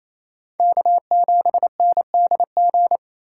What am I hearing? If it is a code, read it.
Morse code, heard as a keyed tone, about 36 wpm, 700 Hz. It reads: K7NDG